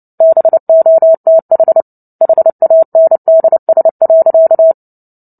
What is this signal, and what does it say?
Morse code, heard as a keyed tone, about 30 wpm, 650 Hz. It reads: BOTH HANDS.
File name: Book Two / 03-Chapter 3 - 29 WPM